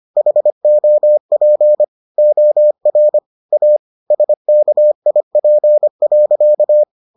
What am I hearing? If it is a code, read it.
Morse code, heard as a keyed tone, about 25 wpm, 600 Hz. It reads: HOP OR A SKIP.